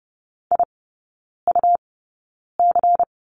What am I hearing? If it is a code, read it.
Morse code, heard as a keyed tone, about 30 wpm, spaced as for 9 wpm, 700 Hz. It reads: IUC
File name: Lesson 18e - Letter m - Three Characters 30wpm